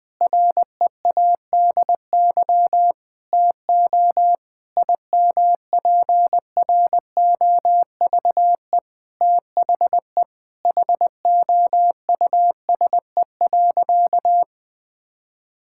Morse code, heard as a keyed tone, about 21 wpm, 700 Hz. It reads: READY TO IMPROVE THE HOUSE.
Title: Sentences from Top 1000 Words - Review Part 5 - 20wpm